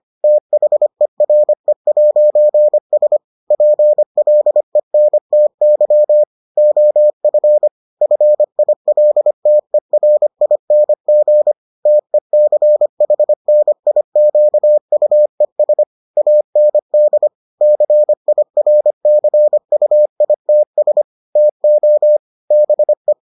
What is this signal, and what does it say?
Morse code, heard as a keyed tone, about 25 wpm, 600 Hz. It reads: THERE'S PLENTY OF FILTERING TECHNIQUES AND CIRCUITS TO BE